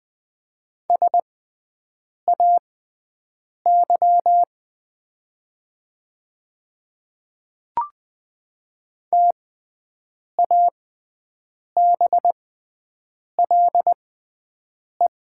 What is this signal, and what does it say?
Morse code, heard as a keyed tone, about 20 wpm, spaced as for 7 wpm, 700 Hz. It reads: SAY TABLE
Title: 6x Character Spacing - Top 500 Words - 20wpm